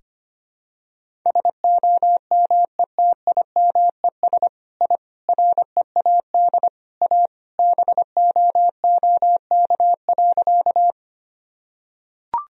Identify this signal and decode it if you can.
Morse code, heard as a keyed tone, about 25 wpm, 700 Hz. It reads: SOMETIMES I READ A BOOK.
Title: Sentences from Top 1000 Words - Review Part 3 - 25wpm